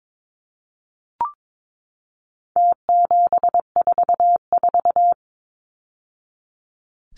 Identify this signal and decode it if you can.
Morse code, heard as a keyed tone, about 22 wpm, 700 Hz. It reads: T744
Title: Numbers - Contest Style - 4-Digit 22wpm